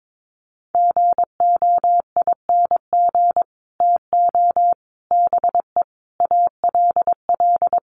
Morse code, heard as a keyed tone, about 22 wpm, 700 Hz. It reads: GOING TO BE ALL
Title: Sets of 5 Words Encore2 22wpm